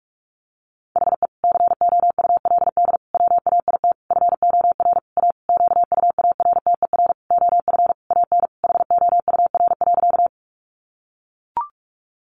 Morse code, heard as a keyed tone, about 44 wpm, 700 Hz. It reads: HE COULD WAIT FOR A QUARTER OF AN HOUR.
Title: Sentences from Top 1300 Words 45wpm